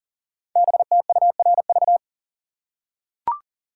Morse code, heard as a keyed tone, about 38 wpm, 690 Hz. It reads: BTURV
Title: Code Groups - 5 Letters - 40wpm